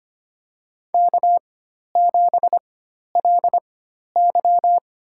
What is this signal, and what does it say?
Morse code, heard as a keyed tone, about 25 wpm, spaced as for 12 wpm, 700 Hz. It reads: K7LY
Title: USA Call Signs - 4 Characters - 4x Character Spacing - 25wpm